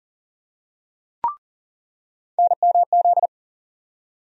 Morse code, heard as a keyed tone, about 40 wpm, 700 Hz. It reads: NMZ
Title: USA Call Sign Postfixes - 3 Letters 40wpm